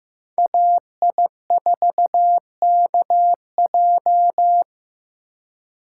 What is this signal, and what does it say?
Morse code, heard as a keyed tone, about 15 wpm, 700 Hz. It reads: AI4KJ
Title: USA Call Signs - 5 Characters - 4x Character Spacing - 15wpm